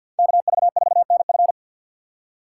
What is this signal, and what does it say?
Morse code, heard as a keyed tone, about 52 wpm, 690 Hz. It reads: KV4NF
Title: USA Call Signs - 5 Characters - 3x Character Spacing - 50wpm